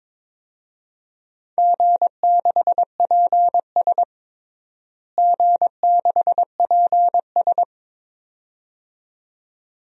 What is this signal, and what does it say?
Morse code, heard as a keyed tone, about 22 wpm, 700 Hz. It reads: G6PS G6PS
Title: DXCC-100 Call Signs - 4 Characters - Repeated 2x - 22wpm